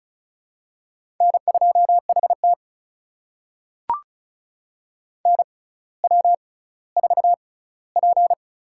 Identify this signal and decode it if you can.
Morse code, heard as a keyed tone, about 36 wpm, spaced as for 18 wpm, 700 Hz. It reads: N2HT NW4P
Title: USA Call Signs - 4 Characters - 6x Character Spacing - 35wpm